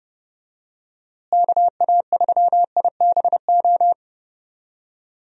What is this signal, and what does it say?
Morse code, heard as a keyed tone, about 30 wpm, 700 Hz. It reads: KA3IBO
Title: USA Call Signs - 6 Characters - Repeated 2x - 30wpm